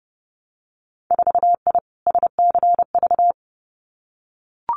Morse code, heard as a keyed tone, about 30 wpm, 700 Hz. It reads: 4I SCV
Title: Winter Field Day Exchanges Encore 30wpm